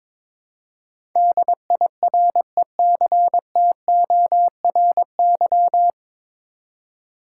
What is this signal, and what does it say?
Morse code, heard as a keyed tone, about 22 wpm, 700 Hz. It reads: DIRECTORY
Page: Morse Code Ninja practice 22wpm